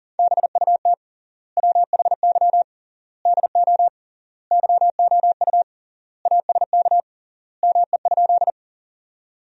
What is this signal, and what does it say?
Morse code, heard as a keyed tone, about 40 wpm, 700 Hz. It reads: BUT WHY DO YOU ASK ME?
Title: 3x Word-Spacing - Sentences from Top 200 Words - 40wpm